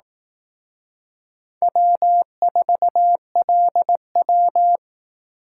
Morse code, heard as a keyed tone, about 18 wpm, 700 Hz. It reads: W4LW
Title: Speed Racing - USA 1x2 Call Signs 15wpm